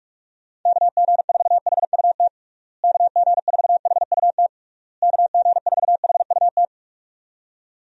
Morse code, heard as a keyed tone, about 46 wpm, 700 Hz. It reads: KG4HUT KG4HUT KG4HUT
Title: USA Call Signs - 6 Characters - Repeated 3x - 45wpm